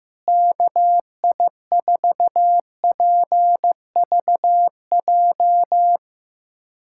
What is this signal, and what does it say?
Morse code, heard as a keyed tone, about 15 wpm, 700 Hz. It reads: KI4PVJ
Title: USA Callsign DE Callsign - 6 Characters - 15wpm